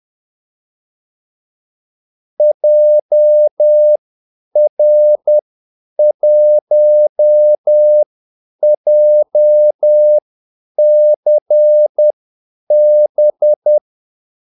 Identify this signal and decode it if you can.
Morse code, heard as a keyed tone, about 10 wpm, spaced as for 8 wpm, 600 Hz. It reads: JR1JCB
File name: Callsigns-10-8wpm